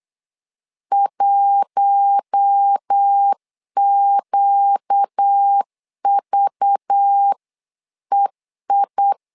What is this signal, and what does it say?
Morse code, heard as a keyed tone, about 8 wpm, 800 Hz. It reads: 1QVEI
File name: LEZIONI / 5 LBVC / 5 LBV 8wpm 2837